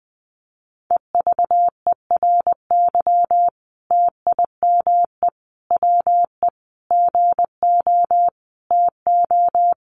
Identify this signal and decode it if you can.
Morse code, heard as a keyed tone, about 19 wpm, 700 Hz. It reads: EVERY TIME WE GO TO
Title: Sentences from Top 1300 Words 20wpm